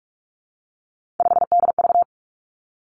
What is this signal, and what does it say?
Morse code, heard as a keyed tone, about 46 wpm, 700 Hz. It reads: 5DV
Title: Three Challenging Letter-Number Combinations 45wpm